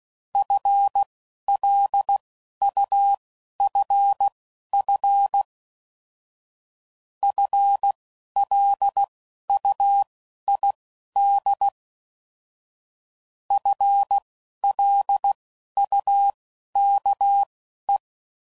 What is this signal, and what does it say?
Morse code, heard as a keyed tone, about 16 wpm, spaced as for 11 wpm, 800 Hz. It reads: FLUFF FLUID FLUKE